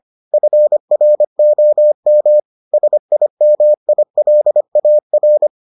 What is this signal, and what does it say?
Morse code, heard as a keyed tone, about 25 wpm, 600 Hz. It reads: FROM SIMILAR